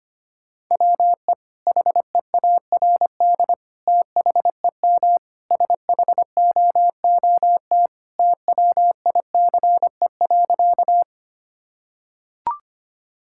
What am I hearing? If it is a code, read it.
Morse code, heard as a keyed tone, about 25 wpm, 700 Hz. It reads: WE HEARD THEM SHOOT TWICE.
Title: Sentences from Top 1600 Words 25wpm